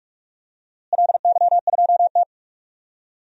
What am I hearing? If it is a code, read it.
Morse code, heard as a keyed tone, about 46 wpm, 700 Hz. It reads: LY2T
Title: DXCC-100 Call Signs - 4 Characters - Repeated 2x - 45wpm